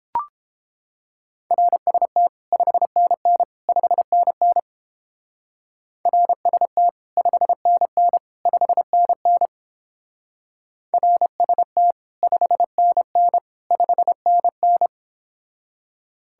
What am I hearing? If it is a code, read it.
Morse code, heard as a keyed tone, about 29 wpm, 700 Hz. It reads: RST 5NN 5NN RST 5NN 5NN RST 5NN 5NN